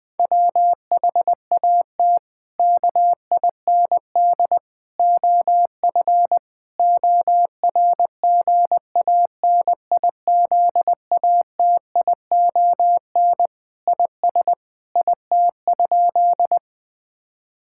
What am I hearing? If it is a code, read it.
Morse code, heard as a keyed tone, about 20 wpm, 700 Hz. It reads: WHAT KIND OF ORGANIZATION IS IT?
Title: Sentences from Top 1000 Words - Review Part 5 - 20wpm